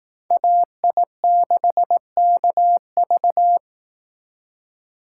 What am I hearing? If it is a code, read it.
Morse code, heard as a keyed tone, about 18 wpm, 700 Hz. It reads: AI6KV
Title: Speed Racing - USA 2x2 Call Signs 15wpm